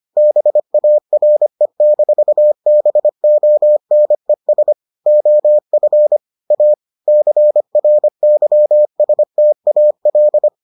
Morse code, heard as a keyed tone, about 25 wpm, 600 Hz. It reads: BARE-BONES OF A CRYSTAL